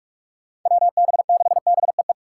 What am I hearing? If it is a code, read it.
Morse code, heard as a keyed tone, about 46 wpm, 700 Hz. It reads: WB6BEE
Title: K1USN SST QSO Practice - 45wpm